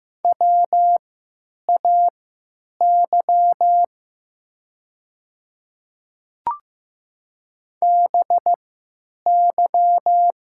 Morse code, heard as a keyed tone, about 15 wpm, spaced as for 9 wpm, 700 Hz. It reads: WAY BY
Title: 3x Character Spacing - Top 100 Words - 15wpm